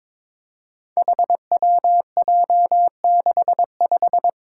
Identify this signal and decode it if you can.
Morse code, heard as a keyed tone, about 22 wpm, 700 Hz. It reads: HWJ65
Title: Five Challenging Letter-Number Combinations 22wpm